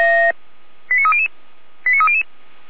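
Five-tone selective call (ZVEI), heard as quick runs of five tones, tones of about 70 ms. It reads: 89300 89300
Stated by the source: ZVEI five-tone call 89350 received by SDR.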